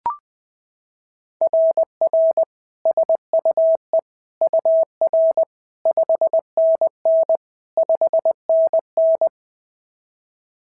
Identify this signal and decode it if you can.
Morse code, heard as a keyed tone, about 20 wpm, 690 Hz. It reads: RR SUE UR 5NN 5NN
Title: SOTA QSO Practice 20wpm